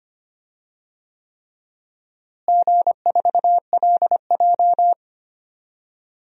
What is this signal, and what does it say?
Morse code, heard as a keyed tone, about 25 wpm, 700 Hz. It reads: G4LJ
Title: DXCC-100 Call Signs - 4 Characters - Repeated 3x - 25wpm